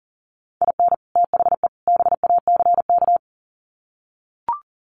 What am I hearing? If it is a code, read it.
Morse code, heard as a keyed tone, about 40 wpm, 700 Hz. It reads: IN THE BACK
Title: Sets of 3 Words 40wpm